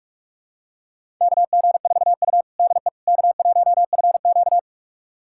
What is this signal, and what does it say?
Morse code, heard as a keyed tone, about 42 wpm, 700 Hz. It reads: KG4U DE K1FQ
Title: USA Callsign DE Callsign - 4 Characters - 45wpm